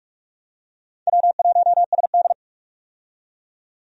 Morse code, heard as a keyed tone, about 46 wpm, 700 Hz. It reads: W1SD
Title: USA Call Signs - 4 Characters - 45wpm